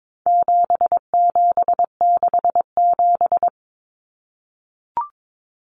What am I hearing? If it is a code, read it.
Morse code, heard as a keyed tone, about 22 wpm, 700 Hz. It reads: 7767
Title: Numbers - 4-Digit 22wpm